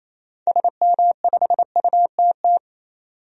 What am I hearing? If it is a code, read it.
Morse code, heard as a keyed tone, about 28 wpm, 700 Hz. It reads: SM5UTT
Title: DXCC-100 Call Signs - 6 Characters - 28wpm